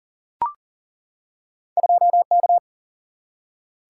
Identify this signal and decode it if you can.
Morse code, heard as a keyed tone, about 40 wpm, 700 Hz. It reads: 2K
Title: Single-Letter Single-Number 40wpm